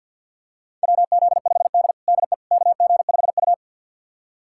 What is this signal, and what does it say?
Morse code, heard as a keyed tone, about 53 wpm, 700 Hz. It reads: WZ5D DE KG5U
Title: USA Callsign DE Callsign - 4 Characters - 50wpm